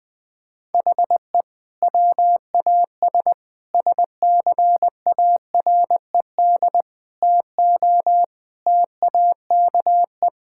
Morse code, heard as a keyed tone, about 20 wpm, 700 Hz. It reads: HE WAS SCARED TO TAKE
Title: Sentences from Top 1000 Words - Review Part 6 - 20wpm